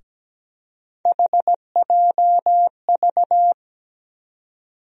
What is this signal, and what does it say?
Morse code, heard as a keyed tone, about 17 wpm, 700 Hz. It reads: HJV
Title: USA Call Sign Postfixes - 3 Letters 17wpm